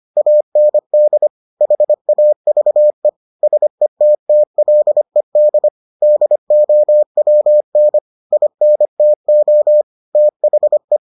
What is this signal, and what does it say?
Morse code, heard as a keyed tone, about 25 wpm, 600 Hz. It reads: AND HAVE SETTLED DOWN INTO THE